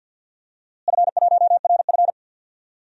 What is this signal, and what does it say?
Morse code, heard as a keyed tone, about 47 wpm, 700 Hz. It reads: F1RF
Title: DXCC-100 Call Signs - 4 Characters - Repeated 2x - 50wpm